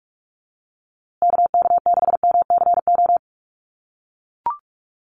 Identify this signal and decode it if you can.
Morse code, heard as a keyed tone, about 44 wpm, 700 Hz. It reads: KK6MCO
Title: USA Call Signs Repeated 3x 45wpm